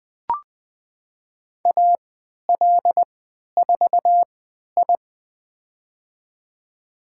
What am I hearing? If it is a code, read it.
Morse code, heard as a keyed tone, about 20 wpm, spaced as for 11 wpm, 700 Hz. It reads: AL4I
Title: USA Call Signs - 4 Characters - 3x Character Spacing - 20wpm